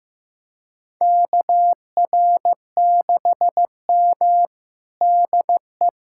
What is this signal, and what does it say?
Morse code, heard as a keyed tone, about 16 wpm, 700 Hz. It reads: KR6M DE
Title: USA Callsign DE Callsign - 4 Characters - 15wpm